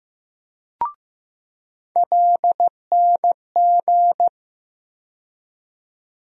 Morse code, heard as a keyed tone, about 15 wpm, 700 Hz. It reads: LNG
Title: USA Call Sign Postfixes - 3 Letters 15wpm